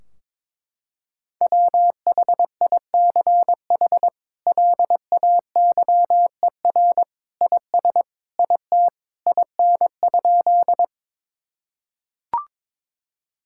Morse code, heard as a keyed tone, about 21 wpm, 700 Hz. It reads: WHICH LAYER IS IT IN?
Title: Sentences from Top 1600 Words 22wpm